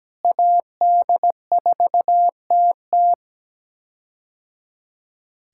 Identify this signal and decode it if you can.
Morse code, heard as a keyed tone, about 17 wpm, 700 Hz. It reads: AD4TT
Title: K1USN SST QSO Practice - 17wpm